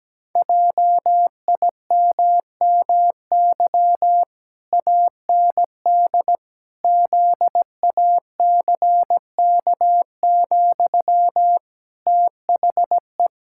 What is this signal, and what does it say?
Morse code, heard as a keyed tone, about 17 wpm, 700 Hz. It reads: JIMMY AND ZACK, THE
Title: Pangram Challenge v2 - 17wpm